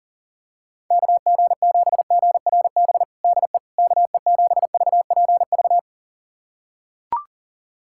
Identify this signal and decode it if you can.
Morse code, heard as a keyed tone, about 40 wpm, 700 Hz. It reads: KG7GRB DE KE7VPV
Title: USA Callsign DE Callsign - 6 Characters - 40wpm